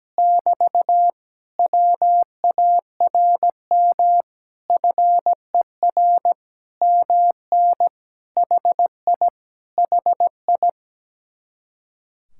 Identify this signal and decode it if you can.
Morse code, heard as a keyed tone, about 17 wpm, 700 Hz. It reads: <BT> WARM FER MN HI HI
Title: Speed-Racing Common QSO Phrases - 17wpm